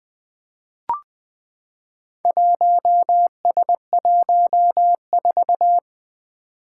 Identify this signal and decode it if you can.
Morse code, heard as a keyed tone, about 20 wpm, 700 Hz. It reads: 1S14
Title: Four Challenging Letter-Number Combinations 20wpm